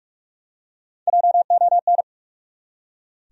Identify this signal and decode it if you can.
Morse code, heard as a keyed tone, about 46 wpm, 700 Hz. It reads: JON
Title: USA Common Names - Max 5 Letters - 45wpm